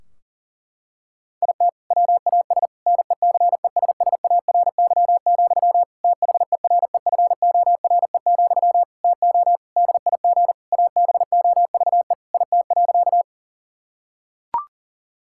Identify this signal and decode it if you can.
Morse code, heard as a keyed tone, about 40 wpm, 700 Hz. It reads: IT WAS NECESSARY, THEREFORE, TO DIG ABOVE IT.
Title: Sentences from Top 1600 Words 40wpm